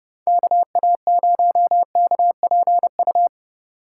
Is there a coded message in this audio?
Yes: Morse code from 0.3 to 3.3 s, heard as a keyed tone, about 29 wpm, 700 Hz. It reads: KA0KPU